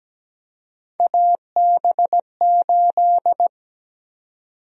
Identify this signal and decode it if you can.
Morse code, heard as a keyed tone, about 17 wpm, 700 Hz. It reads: AB8
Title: USA Call Sign Prefixes - 17wpm